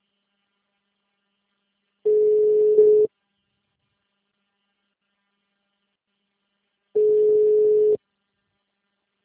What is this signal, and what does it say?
Ringback tone (Europe, 425 Hz).
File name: out- -2011- - - .48490